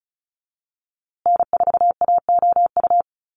Morse code, heard as a keyed tone, about 34 wpm, 700 Hz. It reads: N4AOU
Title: USA Call Signs Repeated 2x 35wpm